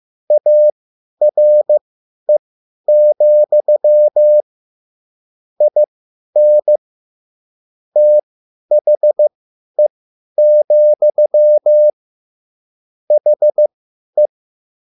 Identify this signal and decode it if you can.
Morse code, heard as a keyed tone, about 15 wpm, spaced as for 10 wpm, 600 Hz. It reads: ARE, IN THE, HE